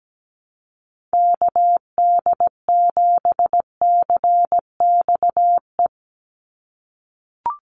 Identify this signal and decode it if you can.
Morse code, heard as a keyed tone, about 17 wpm, 700 Hz. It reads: KD7CXE
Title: USA Call Signs 17wpm